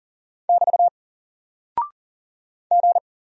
Morse code, heard as a keyed tone, about 38 wpm, 700 Hz. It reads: <BT> G
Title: Single Character (Letter:Number:Punctuation:Prosign) Alt 40wpm